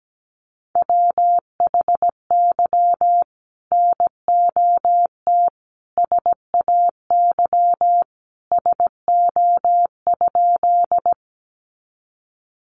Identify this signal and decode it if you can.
Morse code, heard as a keyed tone, about 17 wpm, 700 Hz. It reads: WHY NOT SAY SO?
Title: Sentences from Top 600 Words 17wpm